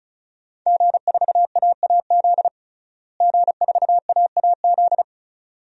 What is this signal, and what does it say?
Morse code, heard as a keyed tone, about 36 wpm, 700 Hz. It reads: G4AAZ G4AAZ
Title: DXCC-100 Call Signs - 5 Characters - Repeated 2x - 35wpm